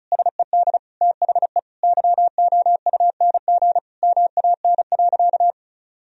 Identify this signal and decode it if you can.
Morse code, heard as a keyed tone, about 36 wpm, 700 Hz. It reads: SED THE YOUNG MAN.
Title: Sentences from Top 400 Words and Abbreviations 35wpm